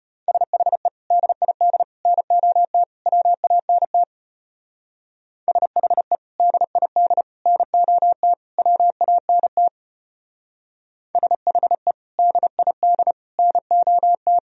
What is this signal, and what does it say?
Morse code, heard as a keyed tone, about 31 wpm, 700 Hz. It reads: SHE DID NOT WANT SHE DID NOT WANT SHE DID NOT
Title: Speed-Racing - Sets of 4 Words 25wpm